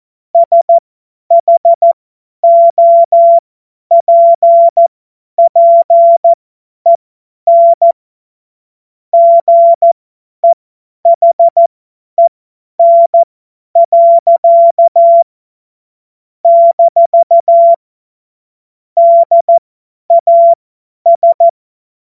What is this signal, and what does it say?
Morse code, heard as a keyed tone, about 14 wpm, spaced as for 10 wpm, 680 Hz. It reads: SHOPPEN GEHEN. - DAS
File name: Morsekurs / Witze / Witz 24. Januar 2023 50 F70